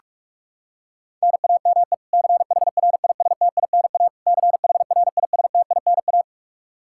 Morse code, heard as a keyed tone, about 46 wpm, 700 Hz. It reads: NAME CHRISTINA CHRISTINA
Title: K1USN SST QSO Practice 45wpm